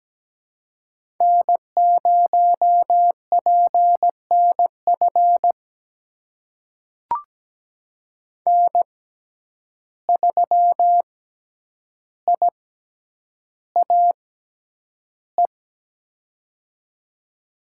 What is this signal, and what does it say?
Morse code, heard as a keyed tone, about 17 wpm, spaced as for 9 wpm, 700 Hz. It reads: N0PNF N3IAE
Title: USA Call Signs - 5 Characters - 6x Character Spacing - 17wpm